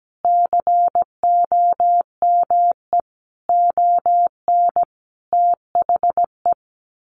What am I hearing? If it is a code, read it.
Morse code, heard as a keyed tone, about 17 wpm, 700 Hz. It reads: COME ON THE
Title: Sets of 5 Words Encore2 17wpm